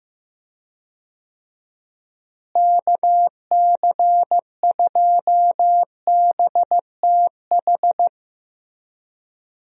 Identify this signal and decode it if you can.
Morse code, heard as a keyed tone, about 15 wpm, 700 Hz. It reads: KC2BTH
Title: USA Call Signs - 6 Characters - Repeated 2x - 15wpm